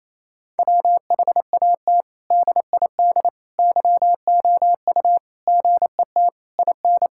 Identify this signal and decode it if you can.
Morse code, heard as a keyed tone, about 28 wpm, 700 Hz. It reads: WHAT DID YOU GET IN
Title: Sentences from Top 1000 Words - Review Part 6 - 28wpm